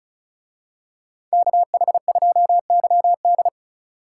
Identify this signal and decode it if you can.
Morse code, heard as a keyed tone, about 34 wpm, 700 Hz. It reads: KH2YD
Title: Speed Racing - USA 2x2 Call Signs 30wpm